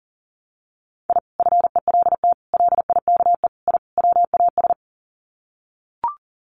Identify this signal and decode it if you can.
Morse code, heard as a keyed tone, about 40 wpm, 700 Hz. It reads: I FELT LIKE I WAS
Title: Sets of 5 Words Encore 40wpm